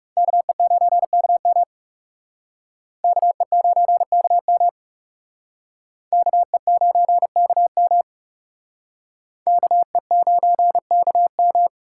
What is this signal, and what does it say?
Morse code, heard as a keyed tone, about 39 wpm, 700 Hz. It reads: KE9KM KE9KM KE9KM KE9KM